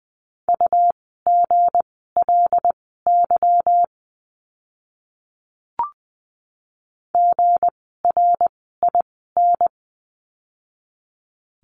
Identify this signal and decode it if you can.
Morse code, heard as a keyed tone, about 20 wpm, spaced as for 15 wpm, 700 Hz. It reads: UGLY GRIN
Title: 160 Lesson 26f - Letter g - Words 20wpm